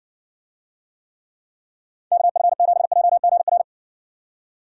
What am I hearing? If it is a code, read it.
Morse code, heard as a keyed tone, about 70 wpm, 700 Hz. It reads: KF7JGR